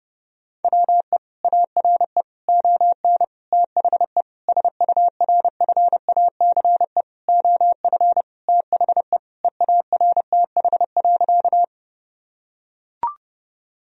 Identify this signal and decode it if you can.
Morse code, heard as a keyed tone, about 30 wpm, 700 Hz. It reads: WE ARE ON THE SURFACE OF THE EARTH.
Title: Sentences from Top 1000 Words - Review Part 3 - 30wpm